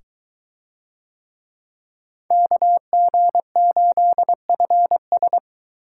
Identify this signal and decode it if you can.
Morse code, heard as a keyed tone, about 22 wpm, 700 Hz. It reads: KG8FS